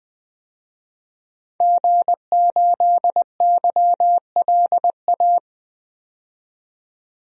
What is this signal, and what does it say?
Morse code, heard as a keyed tone, about 20 wpm, 700 Hz. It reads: G8YLA
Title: DXCC-100 Call Signs - 5 Characters - 20wpm